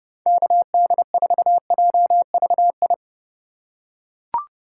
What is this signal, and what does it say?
Morse code, heard as a keyed tone, about 30 wpm, 700 Hz. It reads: KD4JVI
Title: USA Call Signs - 6 Characters - 2x Character Spacing - 30wpm